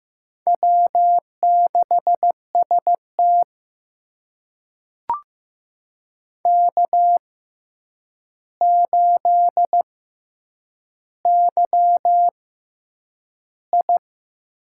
Morse code, heard as a keyed tone, about 15 wpm, spaced as for 8 wpm, 700 Hz. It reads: W6ST K8YI
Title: USA Call Signs - 6x Character Spacing - 15wpm